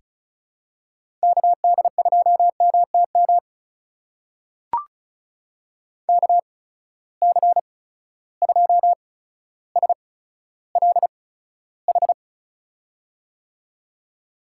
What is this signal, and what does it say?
Morse code, heard as a keyed tone, about 36 wpm, spaced as for 15 wpm, 700 Hz. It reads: KD2MTM KC2SLH
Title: USA Call Signs - 6 Characters - 8x Character Spacing - 35wpm